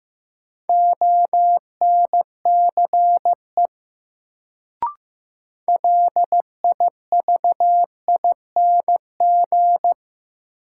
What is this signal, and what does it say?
Morse code, heard as a keyed tone, about 15 wpm, 700 Hz. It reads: ONCE LIVING
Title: Sets of 4 Words - New Word Drill - 15wpm